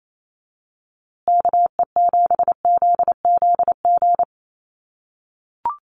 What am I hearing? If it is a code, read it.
Morse code, heard as a keyed tone, about 28 wpm, 700 Hz. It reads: KE7ZZG
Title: USA Call Signs Repeated 2x 28wpm